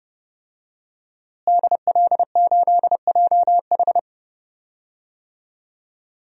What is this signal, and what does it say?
Morse code, heard as a keyed tone, about 30 wpm, 700 Hz. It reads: DL8JH
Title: DXCC-100 Call Signs - 5 Characters - Repeated 3x - 30wpm